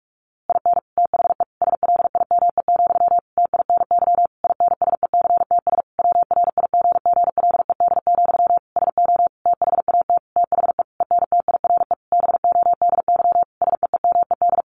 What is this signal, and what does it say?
Morse code, heard as a keyed tone, about 46 wpm, 700 Hz. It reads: IN THE SLIME, TINY INSECTS WRIGGLED, SO THAT THE ENTIRE BODY SEEMED
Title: A Sound Of Thunder 45wpm